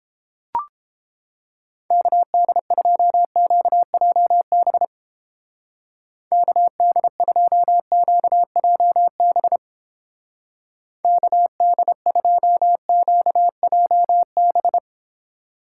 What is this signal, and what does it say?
Morse code, heard as a keyed tone, about 29 wpm, 700 Hz. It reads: KD2QJB KD2QJB KD2QJB